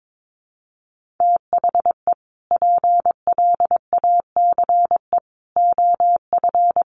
Morse code, heard as a keyed tone, about 22 wpm, 700 Hz. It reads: THE PLACE OF
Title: Sentences from Top 1500 Words 22wpm